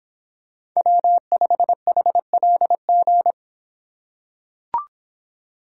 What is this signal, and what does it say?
Morse code, heard as a keyed tone, about 26 wpm, 700 Hz. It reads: W5HLG